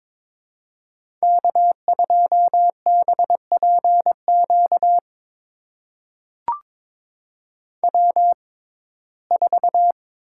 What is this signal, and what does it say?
Morse code, heard as a keyed tone, about 22 wpm, spaced as for 16 wpm, 700 Hz. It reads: K2BPQ W4
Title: USA Call Signs - 5 Characters - 6x Character Spacing - 22wpm